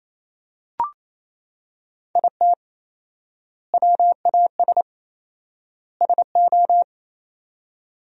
Morse code, heard as a keyed tone, about 28 wpm, 700 Hz. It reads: IT WAS SO
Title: 4x Word-Spacing - Sentences from Top 500 Words - 28wpm